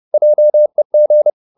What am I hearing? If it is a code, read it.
Morse code, heard as a keyed tone, about 30 wpm, 600 Hz. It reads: JEG
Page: Ferie 30 wpm